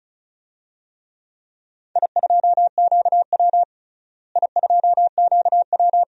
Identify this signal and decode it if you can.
Morse code, heard as a keyed tone, about 36 wpm, 700 Hz. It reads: I2QW I2QW
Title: DXCC-100 Call Signs - 4 Characters - Repeated 2x - 35wpm